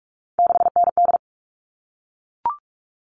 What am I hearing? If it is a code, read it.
Morse code, heard as a keyed tone, about 44 wpm, 700 Hz. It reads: 6ND
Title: All Cut Numbers - 3-Digit 45wpm